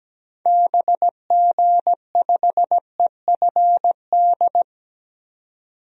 Morse code, heard as a keyed tone, about 17 wpm, 700 Hz. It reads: BG5EFD
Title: DXCC-100 Call Signs - 6 Characters - 17wpm